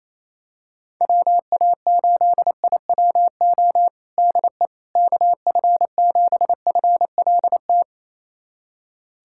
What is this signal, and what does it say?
Morse code, heard as a keyed tone, about 27 wpm, 700 Hz. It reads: WA8IWO DE KF7FLT
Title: USA Callsign DE Callsign 28wpm